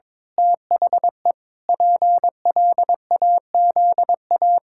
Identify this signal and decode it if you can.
Morse code, heard as a keyed tone, about 22 wpm, 700 Hz. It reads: THE PLAZA
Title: Pangram Challenge v2 - 22wpm